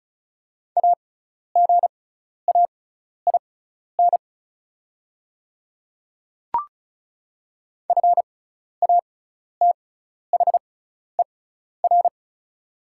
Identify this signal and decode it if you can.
Morse code, heard as a keyed tone, about 34 wpm, spaced as for 12 wpm, 700 Hz. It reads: AGAIN FATHER
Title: 6x Character Spacing - Top 200 Words - 35wpm